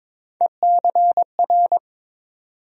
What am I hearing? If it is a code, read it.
Morse code, heard as a keyed tone, about 22 wpm, 700 Hz. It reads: ECR
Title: USA Call Sign Postfixes - 3 Letters 22wpm